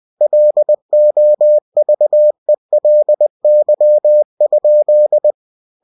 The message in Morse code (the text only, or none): LOVELY?